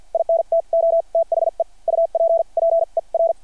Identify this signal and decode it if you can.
Morse code, heard as a keyed tone, about 50 wpm, spaced as for 41 wpm, 660 Hz. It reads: INTO THE UPPER
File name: archived / CRSnet